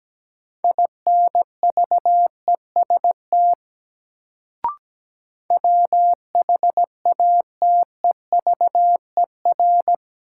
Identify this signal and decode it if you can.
Morse code, heard as a keyed tone, about 17 wpm, 700 Hz. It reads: INVEST WHATEVER
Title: Sentences from Top 1100 Words - New Word Drill Alt - 17wpm